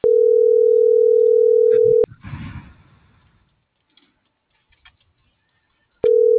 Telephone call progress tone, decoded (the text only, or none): ringback tone